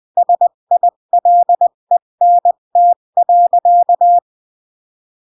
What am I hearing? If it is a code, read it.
Morse code, heard as a keyed tone, about 20 wpm, 700 Hz. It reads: SILENT.
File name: CW / jodorowsky / 82 commandments 20-15